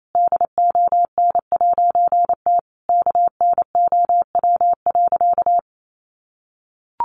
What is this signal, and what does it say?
Morse code, heard as a keyed tone, about 28 wpm, 700 Hz. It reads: DON'T KNOW.
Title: A Sound Of Thunder 28wpm